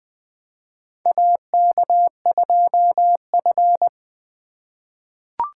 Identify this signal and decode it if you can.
Morse code, heard as a keyed tone, about 20 wpm, 700 Hz. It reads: AK2F
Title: USA Call Signs - 3x Character Spacing - 20wpm